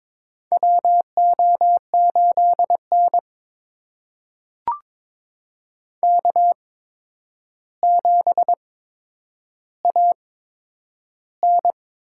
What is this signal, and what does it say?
Morse code, heard as a keyed tone, about 22 wpm, spaced as for 9 wpm, 700 Hz. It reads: WO8N K7AN
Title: USA Call Signs - 4 Characters - 8x Character Spacing - 22wpm